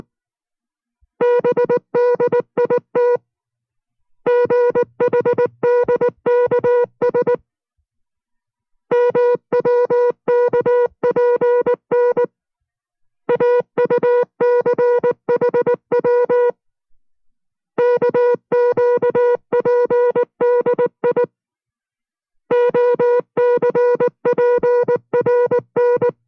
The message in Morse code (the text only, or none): BDIT GHDKS MWKPN AUCHW KQPDI OCPRN